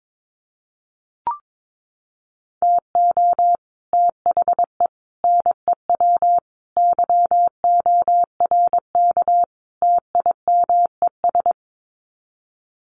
Morse code, heard as a keyed tone, about 22 wpm, 700 Hz. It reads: TO THE NEW YORK TIMES
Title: Sets of 5 Words Encore2 22wpm